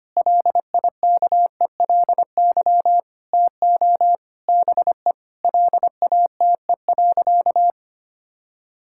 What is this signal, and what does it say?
Morse code, heard as a keyed tone, about 25 wpm, 700 Hz. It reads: LIKELY TO BE LATE.
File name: practice / 25wpm / Sentences with 4-Word Phrases - 25wpm